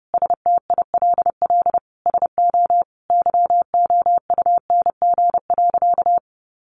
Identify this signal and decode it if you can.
Morse code, heard as a keyed tone, about 30 wpm, 700 Hz. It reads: STILL SO YOUNG.
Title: Sentences from Top 400 Words 30wpm